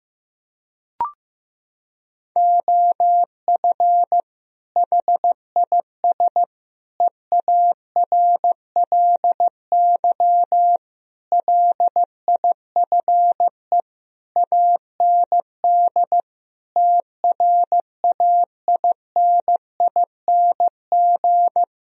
Morse code, heard as a keyed tone, about 15 wpm, 700 Hz. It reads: OF HIS EARLY LIFE AND TRAINING
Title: Sentences from Top 1000 Words - Review Part 4 - 15wpm